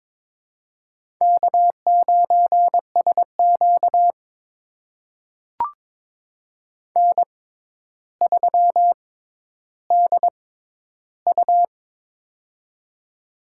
Morse code, heard as a keyed tone, about 22 wpm, spaced as for 11 wpm, 700 Hz. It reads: K9SQ N3DU
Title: USA Call Signs - 4 Characters - 6x Character Spacing - 22wpm